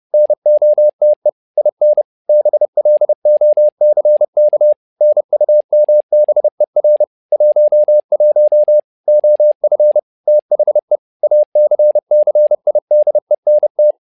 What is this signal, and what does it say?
Morse code, heard as a keyed tone, about 30 wpm, 600 Hz. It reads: NOTE IN BLOCK NUMBER 11 OF THE ACCIDENT
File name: learning center / tower trouble / tower trouble 30wpm